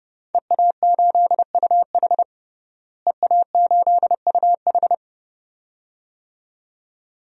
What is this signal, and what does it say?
Morse code, heard as a keyed tone, about 30 wpm, 700 Hz. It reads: EA8UH EA8UH
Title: DXCC-100 Call Signs - 5 Characters - Repeated 2x - 30wpm